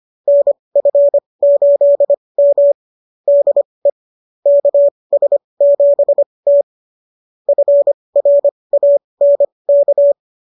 Morse code, heard as a keyed tone, about 25 wpm, spaced as for 20 wpm, 590 Hz. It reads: NF8M DE KS7T FRANK
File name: qso221 20